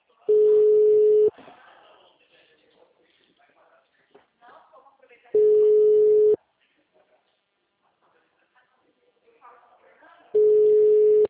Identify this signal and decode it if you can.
Ringback tone (Europe, 425 Hz), with speech.
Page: Itatiba - SP